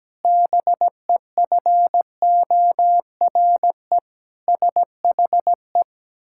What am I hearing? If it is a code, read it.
Morse code, heard as a keyed tone, about 17 wpm, 700 Hz. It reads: BEFORE SHE
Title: Pangram Challenge v2 - 17wpm